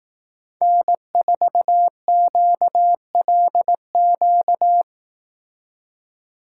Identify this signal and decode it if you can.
Morse code, heard as a keyed tone, about 18 wpm, 700 Hz. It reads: N4QLQ
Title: Speed Racing - USA 1x3 Call Signs 15wpm